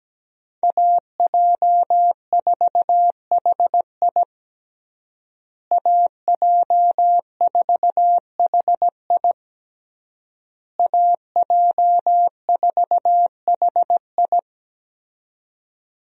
Morse code, heard as a keyed tone, about 17 wpm, 700 Hz. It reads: AJ4HI AJ4HI AJ4HI
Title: USA Call Signs - 5 Characters - Repeated 3x - 17wpm